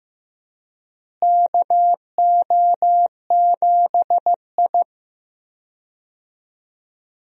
Morse code, heard as a keyed tone, about 15 wpm, 700 Hz. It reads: KO7I
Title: USA Call Signs - 4 Characters - Repeated 2x - 15wpm